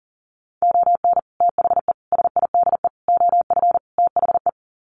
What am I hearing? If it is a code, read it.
Morse code, heard as a keyed tone, about 40 wpm, 700 Hz. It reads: ON THE SIDE OF THE
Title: Sets of 5 Words 40wpm